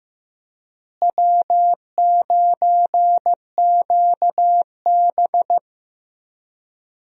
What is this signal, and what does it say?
Morse code, heard as a keyed tone, about 15 wpm, 700 Hz. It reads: W9QB
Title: USA Call Signs - 4 Characters - Repeated 2x - 15wpm